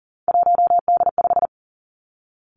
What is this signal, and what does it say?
Morse code, heard as a keyed tone, about 40 wpm, 700 Hz. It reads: 1D5
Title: Three Challenging Letter-Number Combinations 40wpm